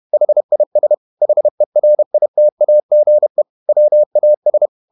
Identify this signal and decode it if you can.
Morse code, heard as a keyed tone, about 31 wpm, 600 Hz. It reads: HIS HERITAGE WAS
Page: Never 31wpm | CW med Gnister